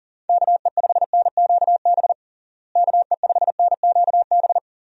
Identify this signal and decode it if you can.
Morse code, heard as a keyed tone, about 40 wpm, 700 Hz. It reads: KE5NQB KE5NQB
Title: USA Call Signs - 6 Characters - Repeated 2x - 40wpm